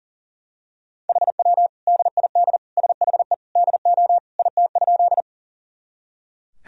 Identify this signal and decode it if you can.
Morse code, heard as a keyed tone, about 37 wpm, 700 Hz. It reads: HW DID SHE DO IT?
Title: Sentences from Top 100 Words and Abbreviations 40wpm